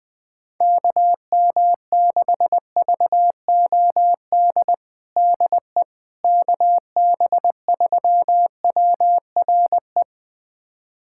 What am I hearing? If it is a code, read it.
Morse code, heard as a keyed tone, about 20 wpm, 700 Hz. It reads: KM6VOD DE KB3WRE
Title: USA Callsign DE Callsign 20wpm